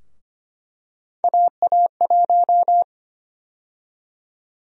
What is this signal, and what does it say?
Morse code, heard as a keyed tone, about 25 wpm, 700 Hz. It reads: AA1
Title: USA Call Sign Prefixes - Repeated 2x - 25wpm